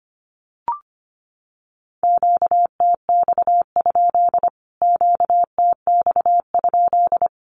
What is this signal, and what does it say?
Morse code, heard as a keyed tone, about 25 wpm, 700 Hz. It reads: QTX? QTX?
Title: Q-Codes Repeated 3x 25wpm